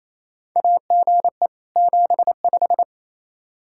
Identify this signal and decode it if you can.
Morse code, heard as a keyed tone, about 28 wpm, 700 Hz. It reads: AGE 75
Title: K1USN SST QSO Practice 28wpm